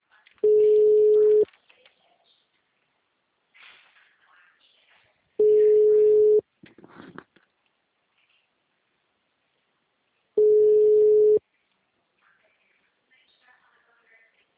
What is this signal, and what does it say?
Ringback tone (Europe, 425 Hz), with speech.
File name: out- -2004- - - .14